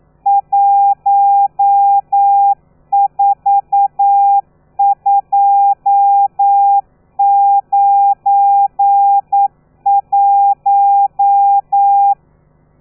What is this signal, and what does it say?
Morse code, heard as a keyed tone, about 9 wpm, 800 Hz. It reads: 14291
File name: telegrafia / i2tfj / Lez12 40